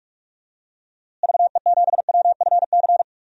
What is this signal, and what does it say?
Morse code, heard as a keyed tone, about 46 wpm, 700 Hz. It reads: VE7WFC
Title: DXCC-100 Call Signs - 6 Characters - Repeated 3x - 45wpm